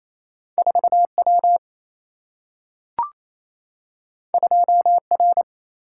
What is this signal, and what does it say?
Morse code, heard as a keyed tone, about 28 wpm, 700 Hz. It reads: 4 W 2 R
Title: Single-Letter Single-Number 28wpm